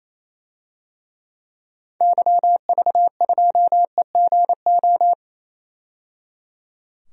Morse code, heard as a keyed tone, about 28 wpm, 700 Hz. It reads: YV2EGO
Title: DXCC-100 Call Signs - 6 Characters - 28wpm